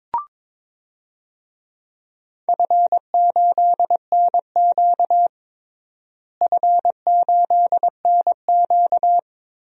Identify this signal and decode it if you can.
Morse code, heard as a keyed tone, about 22 wpm, 700 Hz. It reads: F8NQ F8NQ
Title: DXCC-100 Call Signs - 4 Characters - Repeated 3x - 22wpm